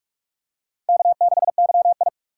Morse code, heard as a keyed tone, about 44 wpm, 700 Hz. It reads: K6YI
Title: USA Call Signs - 4 Characters - 4x Character Spacing - 45wpm